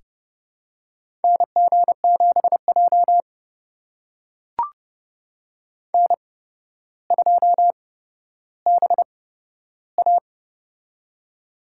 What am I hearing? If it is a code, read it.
Morse code, heard as a keyed tone, about 30 wpm, spaced as for 13 wpm, 700 Hz. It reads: NG7J N2BA
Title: USA Call Signs - 4 Characters - 8x Character Spacing - 30wpm